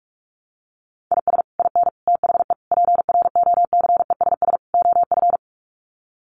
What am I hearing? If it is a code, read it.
Morse code, heard as a keyed tone, about 44 wpm, 700 Hz. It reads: IS IN THE PROCESS OF
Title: Sets of 5 Words Encore2 45wpm